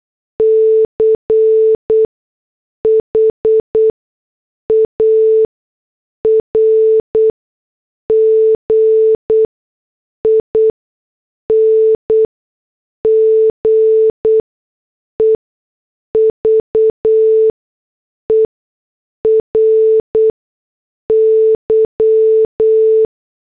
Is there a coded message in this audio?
Yes: Morse code from 0.4 to 23.1 s, heard as a keyed tone, about 8 wpm, spaced as for 6 wpm, 440 Hz. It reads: CHARGING EVERY